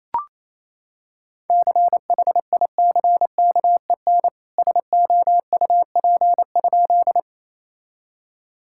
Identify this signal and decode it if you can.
Morse code, heard as a keyed tone, about 28 wpm, 700 Hz. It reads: CHICKEN SOUP?
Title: Sentences from Top 1000 Words - Review Part 6 - 28wpm